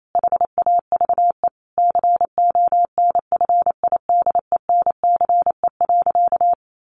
Morse code, heard as a keyed tone, about 28 wpm, 700 Hz. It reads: HAVE CONFIDENCE.
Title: Sentences from Top 1500 Words 28wpm